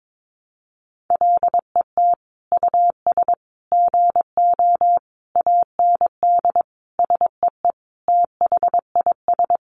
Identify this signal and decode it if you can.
Morse code, heard as a keyed tone, about 22 wpm, 700 Hz. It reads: LET US GO AND SEE THIS
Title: Sentences from Top 1100 Words 22wpm